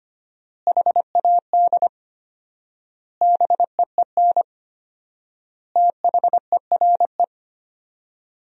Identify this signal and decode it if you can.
Morse code, heard as a keyed tone, about 25 wpm, 700 Hz. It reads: HAD BEEN THERE
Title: 4x Word-Spacing - Sentences from Top 400 Words - 25wpm